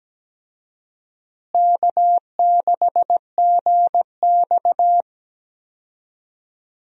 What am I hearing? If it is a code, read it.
Morse code, heard as a keyed tone, about 17 wpm, 700 Hz. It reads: K6GX
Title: USA Call Signs - 4 Characters - Repeated 2x - 17wpm